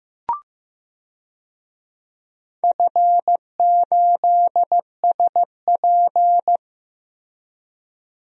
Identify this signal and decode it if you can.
Morse code, heard as a keyed tone, about 15 wpm, 700 Hz. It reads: F8SP
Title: DXCC-100 Call Signs - 4 Characters - 15wpm